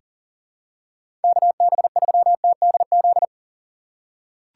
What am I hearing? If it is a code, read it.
Morse code, heard as a keyed tone, about 40 wpm, 700 Hz. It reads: KB3TDZ
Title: USA Call Signs - 6 Characters - 4x Character Spacing - 40wpm